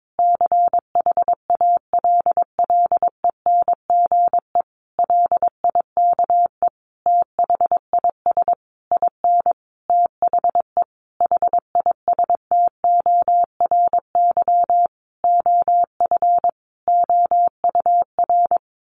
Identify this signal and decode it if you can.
Morse code, heard as a keyed tone, about 22 wpm, 700 Hz. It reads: CHALLENGE LIKE THIS IN THE HISTORY OF OUR
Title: Sentences from Top 1500 Words 22wpm